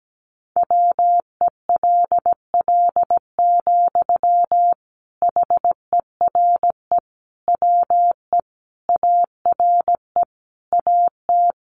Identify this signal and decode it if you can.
Morse code, heard as a keyed tone, about 17 wpm, 700 Hz. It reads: WELL, HERE WE ARE AT
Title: Sentences from Top 300 Words 17wpm